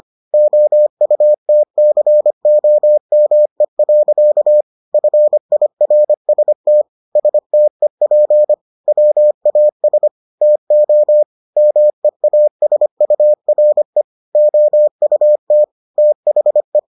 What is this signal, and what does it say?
Morse code, heard as a keyed tone, about 25 wpm, 600 Hz. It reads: OUTCOME. FIRST STEP WAS TO MEASURE OUT THE